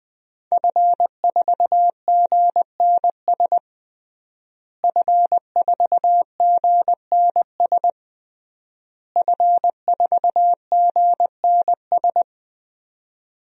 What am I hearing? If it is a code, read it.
Morse code, heard as a keyed tone, about 20 wpm, 700 Hz. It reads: F4GNS F4GNS F4GNS
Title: DXCC-100 Call Signs - 5 Characters - Repeated 3x - 20wpm